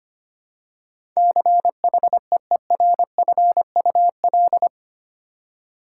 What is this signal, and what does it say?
Morse code, heard as a keyed tone, about 25 wpm, 700 Hz. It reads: CHEERFUL
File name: Eight Letter Words - 25wpm